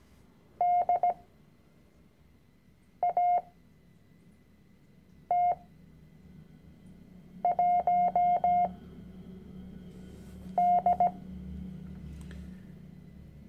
Morse code, heard as a keyed tone, about 17 wpm, spaced as for 4 wpm, 710 Hz. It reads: DAT1D
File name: CW-Online-Kurs / Anfänger / Lektion 010